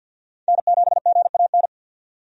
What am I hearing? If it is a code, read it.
Morse code, heard as a keyed tone, about 51 wpm, 700 Hz. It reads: N7GAN
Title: USA Call Signs - 5 Characters - 50wpm